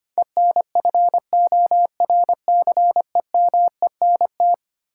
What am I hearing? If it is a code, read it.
Morse code, heard as a keyed tone, about 25 wpm, 700 Hz. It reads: ENFORCEMENT
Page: morse-code-ninja-practice-25wpm